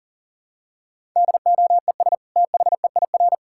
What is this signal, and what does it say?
Morse code, heard as a keyed tone, about 37 wpm, 700 Hz. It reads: DOES THEIR
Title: Sentences from Top 1000 Words - Review Part 5 - 40wpm